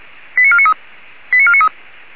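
Five-tone selective call (ZVEI), heard as quick runs of five tones, tones of about 70 ms. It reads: 89493 89493
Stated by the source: ZVEI five-tone selective call received on SDR.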